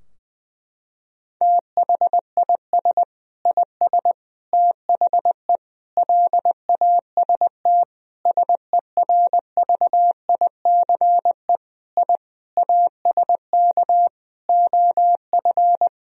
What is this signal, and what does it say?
Morse code, heard as a keyed tone, about 21 wpm, 700 Hz. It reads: THIS IS THE LAST SERVICE I ASK OF
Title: Sentences from Top 1000 Words - Review Part 4 - 20wpm